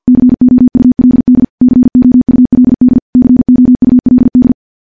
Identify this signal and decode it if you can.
Morse code, heard as a keyed tone, about 50 wpm, 250 Hz. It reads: COULD COULD COULD
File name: morse / morse-training-50WPM